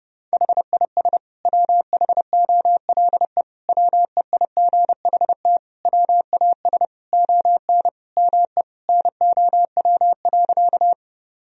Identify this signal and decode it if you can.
Morse code, heard as a keyed tone, about 32 wpm, 700 Hz. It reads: HIS WHOLE WEIGHT WAS ON ME NOW.
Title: Sentences from Top 1000 Words - Review Part 3 - 30wpm